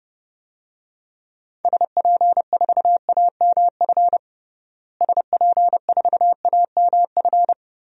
Morse code, heard as a keyed tone, about 32 wpm, 700 Hz. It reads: SP4AMF SP4AMF
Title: DXCC-100 Call Signs - 6 Characters - Repeated 2x - 30wpm